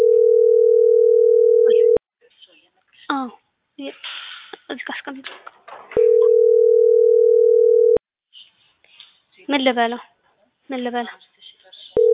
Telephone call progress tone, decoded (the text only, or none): ringback tone